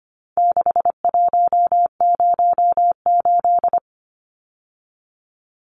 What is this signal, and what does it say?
Morse code, heard as a keyed tone, about 25 wpm, 700 Hz. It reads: 6108
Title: Numbers - 4-Digit 25wpm